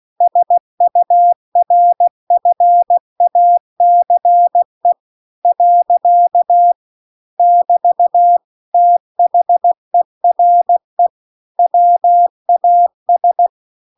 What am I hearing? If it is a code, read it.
Morse code, heard as a keyed tone, about 16 wpm, 700 Hz. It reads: SURFACE . <BT> THERE WAS